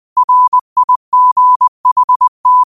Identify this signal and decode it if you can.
Morse code, heard as a keyed tone, about 20 wpm, 1000 Hz. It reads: RIGHT
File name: qotdfiles / 20wpm